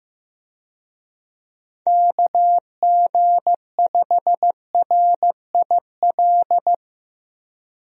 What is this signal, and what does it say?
Morse code, heard as a keyed tone, about 16 wpm, 700 Hz. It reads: KG5RIL
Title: USA Call Signs - 6 Characters - 15wpm